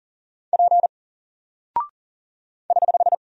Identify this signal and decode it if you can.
Morse code, heard as a keyed tone, about 40 wpm, 700 Hz. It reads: P <HH>
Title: Single Character (Letter:Number:Punctuation:Prosign) 40wpm